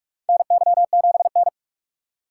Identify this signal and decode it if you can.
Morse code, heard as a keyed tone, about 43 wpm, 700 Hz. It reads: NY7N
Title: K1USN SST QSO Practice - 45wpm